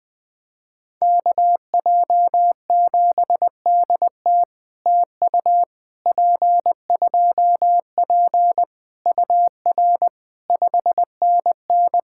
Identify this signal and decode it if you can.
Morse code, heard as a keyed tone, about 20 wpm, 700 Hz. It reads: KJ7DT TU P2P UR 5NN
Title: POTA QSO Practice - 20wpm